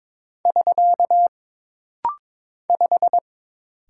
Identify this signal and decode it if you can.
Morse code, heard as a keyed tone, about 22 wpm, 700 Hz. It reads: <SK> 5
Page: This practice set is composed of Letters, Numbers, common punctuations, and common prosigns.